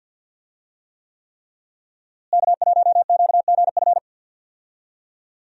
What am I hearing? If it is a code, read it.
Morse code, heard as a keyed tone, about 49 wpm, 700 Hz. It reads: K1QGF
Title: USA Call Signs - 5 Characters - Repeated 3x - 50wpm